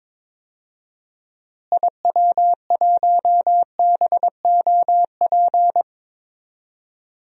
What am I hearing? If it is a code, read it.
Morse code, heard as a keyed tone, about 22 wpm, 700 Hz. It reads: IW1BOP
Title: DXCC-100 Call Signs - 6 Characters - Repeated 2x - 22wpm